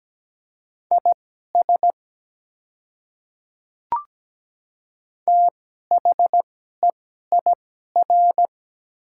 Morse code, heard as a keyed tone, about 17 wpm, spaced as for 12 wpm, 700 Hz. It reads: IS THEIR
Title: 2x Character Spacing - Top 100 Words - 17wpm